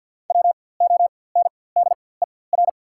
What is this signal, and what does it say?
Morse code, heard as a keyed tone, about 48 wpm, spaced as for 23 wpm, 690 Hz. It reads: WONDER
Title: 4x Character Spacing - Top 500 Words - 50wpm